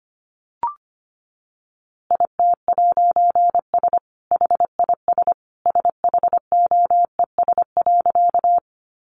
Morse code, heard as a keyed tone, about 25 wpm, 700 Hz. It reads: IT'S HIS SHOES.
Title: A Sound Of Thunder 25wpm